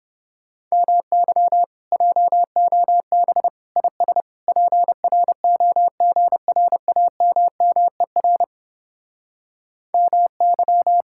Morse code, heard as a keyed tone, about 31 wpm, 700 Hz. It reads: MY JOB IS PROGRAMMER MY